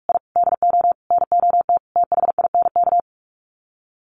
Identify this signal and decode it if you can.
Morse code, heard as a keyed tone, about 44 wpm, 700 Hz. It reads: I DO NOT THINK
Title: Sets of 5 Words Encore 45wpm